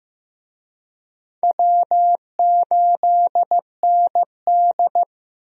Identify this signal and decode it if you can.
Morse code, heard as a keyed tone, about 15 wpm, 700 Hz. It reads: W8ND
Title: USA Call Signs - 4 Characters - 15wpm